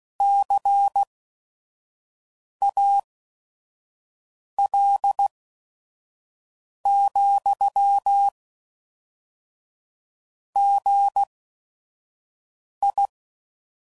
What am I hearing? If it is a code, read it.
Morse code, heard as a keyed tone, about 16 wpm, spaced as for 5 wpm, 800 Hz. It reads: CAL, GI
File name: CW Code Groups @5WPM / Technical Subjects / XC